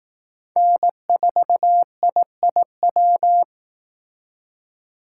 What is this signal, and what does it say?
Morse code, heard as a keyed tone, about 18 wpm, 700 Hz. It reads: N4IIW
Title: Speed Racing - USA 1x3 Call Signs 15wpm